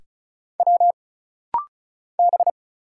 Morse code, heard as a keyed tone, about 36 wpm, 700 Hz. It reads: W B
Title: Single Letters - Rapid-Fire 35wpm